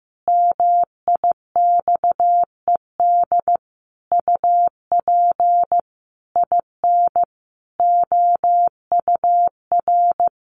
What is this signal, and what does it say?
Morse code, heard as a keyed tone, about 15 wpm, 700 Hz. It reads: MIXED UP IN OUR
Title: Sentences from Top 1500 Words 15wpm